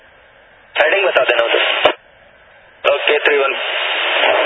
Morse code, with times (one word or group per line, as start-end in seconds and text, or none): none